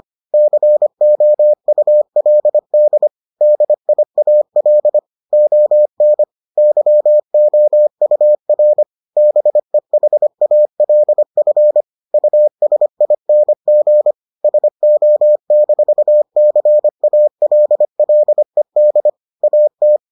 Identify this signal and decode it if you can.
Morse code, heard as a keyed tone, about 25 wpm, 600 Hz. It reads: COULD DIAL ON YOUR BEHALF USING SO-CALLED AT